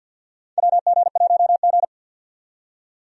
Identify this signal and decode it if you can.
Morse code, heard as a keyed tone, about 52 wpm, 700 Hz. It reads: WG1G
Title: USA Call Signs - 4 Characters - 50wpm